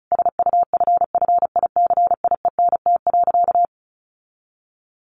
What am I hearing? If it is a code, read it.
Morse code, heard as a keyed tone, about 34 wpm, 700 Hz. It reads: SUFFICIENT.
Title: Sentences from Top 1400 Words 35wpm